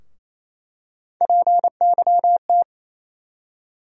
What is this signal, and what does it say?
Morse code, heard as a keyed tone, about 27 wpm, 700 Hz. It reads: PYT
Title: USA Call Sign Postfixes - 3 Letters 28wpm